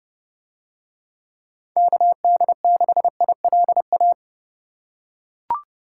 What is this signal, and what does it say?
Morse code, heard as a keyed tone, about 28 wpm, 700 Hz. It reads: KD6ILA